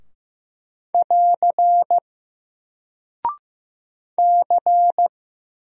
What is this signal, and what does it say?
Morse code, heard as a keyed tone, about 15 wpm, 700 Hz. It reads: <AR> C
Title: Single Character (Letter:Number:Punctuation:Prosign) 15wpm